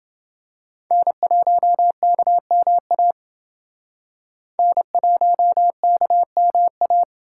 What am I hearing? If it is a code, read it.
Morse code, heard as a keyed tone, about 28 wpm, 700 Hz. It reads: N1KMA N1KMA